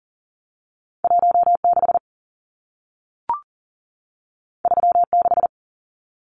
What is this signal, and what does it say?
Morse code, heard as a keyed tone, about 40 wpm, 700 Hz. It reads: 16 36
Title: Numbers - 2-Digit 40wpm